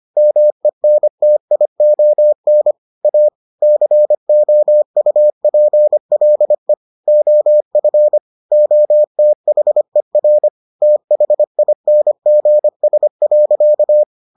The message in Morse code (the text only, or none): MENTION A COUPLE OF OTHER THINGS.